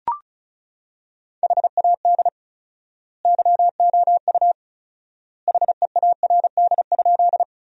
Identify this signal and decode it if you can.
Morse code, heard as a keyed tone, about 36 wpm, 700 Hz. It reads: HAD YOU HEARD?
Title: 4x Word-Spacing - Sentences from Top 400 Words - 35wpm